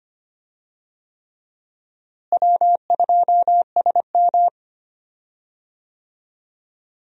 Morse code, heard as a keyed tone, about 25 wpm, 700 Hz. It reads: W2SM
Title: USA Call Signs - 4 Characters - Repeated 3x - 25wpm